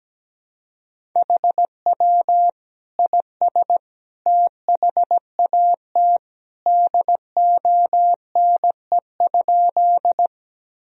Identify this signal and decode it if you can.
Morse code, heard as a keyed tone, about 17 wpm, 700 Hz. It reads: HW IS THAT DONE?
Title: Sentences from Top 500 Words and Abbreviations 17wpm